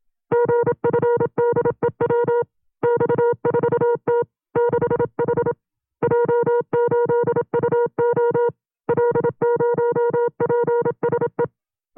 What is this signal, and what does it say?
Morse code, heard as a keyed tone, about 27 wpm, 480 Hz. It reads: GFDEW X4T 6H J8UO L0PSE
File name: CW-practice/25WPM